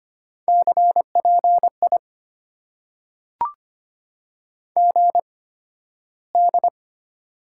Morse code, heard as a keyed tone, about 25 wpm, spaced as for 13 wpm, 690 Hz. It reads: CPI GD
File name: Top 90 QSO Elements - 8x Character Spacing - 25wpm